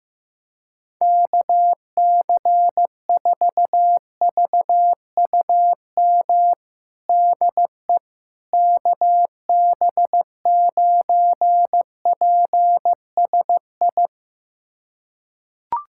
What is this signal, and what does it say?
Morse code, heard as a keyed tone, about 15 wpm, 700 Hz. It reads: KC4VUM DE KB9PSI
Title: USA Callsign DE Callsign - 6 Characters - 15wpm